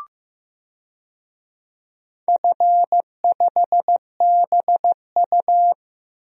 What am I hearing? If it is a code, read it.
Morse code, heard as a keyed tone, about 15 wpm, 700 Hz. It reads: F5BU
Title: DXCC-100 Call Signs - 4 Characters - Repeated 2x - 15wpm